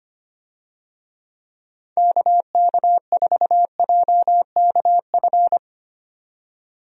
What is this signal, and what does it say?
Morse code, heard as a keyed tone, about 25 wpm, 700 Hz. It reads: KK4JKF
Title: USA Call Signs - 6 Characters - 25wpm